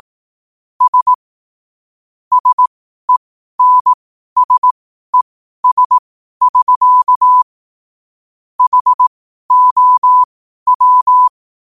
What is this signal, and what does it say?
Morse code, heard as a keyed tone, about 18 wpm, spaced as for 12 wpm, 1000 Hz. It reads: S SENSES<SK> HOW
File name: qotdfiles / 12wpm